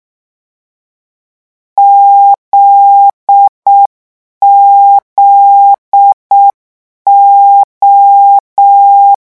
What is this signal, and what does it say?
Morse code, heard as a keyed tone, about 6 wpm, 800 Hz. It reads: ZZO
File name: morse mp3 / lecon9-vitesse 6